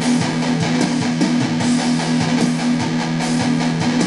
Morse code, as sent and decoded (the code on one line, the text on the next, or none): none